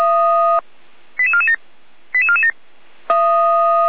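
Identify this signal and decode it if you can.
Five-tone selective call (ZVEI), heard as quick runs of five tones, tones of about 80 ms. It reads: 88447 88447
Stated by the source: ZVEI five-tone call 88247 received by SDR, logged by Feuerwehr Orlamuende (f85360).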